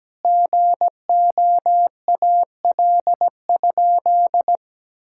Morse code, heard as a keyed tone, about 17 wpm, 700 Hz. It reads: GOAL?
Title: Sentences from Top 1000 Words - Review Part 6 - 17wpm